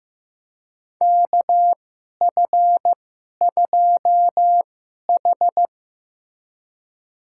Morse code, heard as a keyed tone, about 15 wpm, spaced as for 11 wpm, 700 Hz. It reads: KF2H
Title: USA Call Signs - 4 Characters - 2x Character Spacing - 15wpm